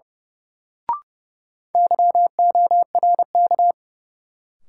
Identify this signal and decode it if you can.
Morse code, heard as a keyed tone, about 30 wpm, 700 Hz. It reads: YORK
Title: Sets of 2 Words - New Word Drill - 30wpm